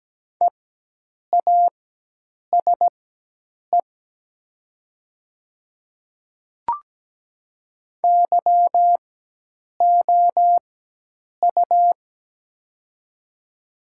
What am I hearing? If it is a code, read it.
Morse code, heard as a keyed tone, about 17 wpm, spaced as for 8 wpm, 700 Hz. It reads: EASE YOU
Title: 4x Character Spacing - Top 300 Words - 17wpm